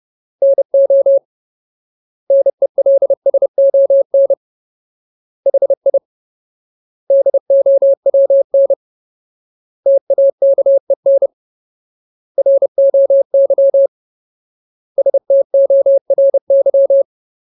Morse code, heard as a keyed tone, about 30 wpm, 550 Hz. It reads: NO NELSON HI DOWN TAKEN ROY STORY
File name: Ultimate Word List / 30WPM Ultimate